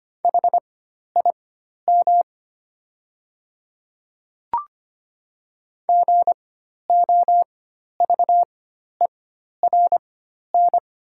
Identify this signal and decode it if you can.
Morse code, heard as a keyed tone, about 25 wpm, spaced as for 12 wpm, 690 Hz. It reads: HIM GOVERN
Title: 4x Character Spacing - Top 500 Words - 25wpm